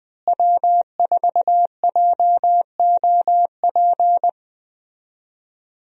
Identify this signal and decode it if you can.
Morse code, heard as a keyed tone, about 20 wpm, 700 Hz. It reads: W4JOP
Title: USA Call Signs - 5 Characters - Repeated 3x - 20wpm